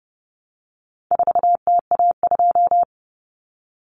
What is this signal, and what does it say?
Morse code, heard as a keyed tone, about 30 wpm, 700 Hz. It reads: 4TA2
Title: Common Cut Numbers - 4-Digit 30wpm